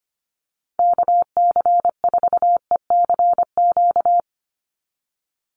Morse code, heard as a keyed tone, about 25 wpm, 700 Hz. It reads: KC4ECQ
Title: USA Call Signs 25wpm